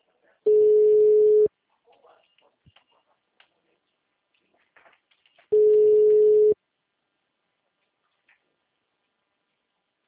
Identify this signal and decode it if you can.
Ringback tone (Europe, 425 Hz), with speech.